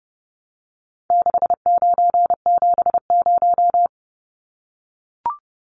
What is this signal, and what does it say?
Morse code, heard as a keyed tone, about 30 wpm, 700 Hz. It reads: 6970
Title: Numbers - 4-Digit 30wpm